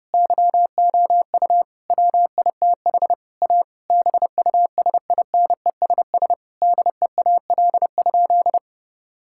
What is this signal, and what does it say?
Morse code, heard as a keyed tone, about 32 wpm, 700 Hz. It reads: YOU WITH A BUSINESS DEAL?
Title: Sentences from Top 1000 Words - Review Part 6 - 30wpm